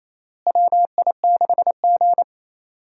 Morse code, heard as a keyed tone, about 28 wpm, 700 Hz. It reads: WI6G
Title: USA Call Signs - 4 Characters - Repeated 3x - 28wpm